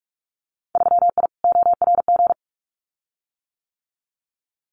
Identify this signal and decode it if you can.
Morse code, heard as a keyed tone, about 44 wpm, 700 Hz. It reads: 3I ORG
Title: Winter Field Day Exchanges Encore 45wpm